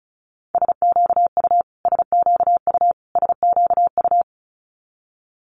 Morse code, heard as a keyed tone, about 36 wpm, 700 Hz. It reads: SQU SQU SQU
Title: Multi-letter Phonemes Repeated 3x 35wpm